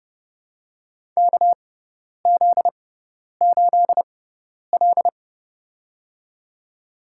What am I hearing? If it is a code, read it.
Morse code, heard as a keyed tone, about 32 wpm, spaced as for 10 wpm, 700 Hz. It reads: KZ8L
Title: USA Call Signs - 4 Characters - 6x Character Spacing - 30wpm